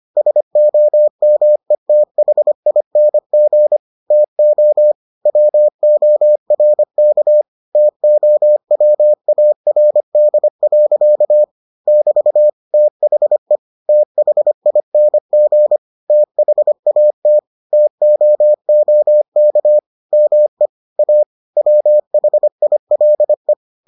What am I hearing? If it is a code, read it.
Morse code, heard as a keyed tone, about 25 wpm, 600 Hz. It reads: SOMETHING TO WORK TOWARD. <BT> THE THING THAT TOOK ME A WHILE